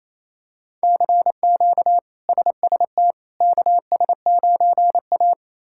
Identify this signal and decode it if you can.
Morse code, heard as a keyed tone, about 28 wpm, 700 Hz. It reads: CQ SST KS9A
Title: K1USN SST QSO Practice - 28wpm